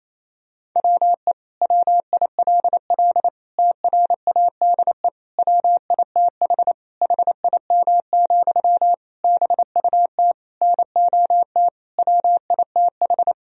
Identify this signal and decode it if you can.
Morse code, heard as a keyed tone, about 28 wpm, 700 Hz. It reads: WE WILL TRADE WITH HIM, BUT NOT WITH
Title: Sentences from Top 1000 Words - Review Part 4 - 28wpm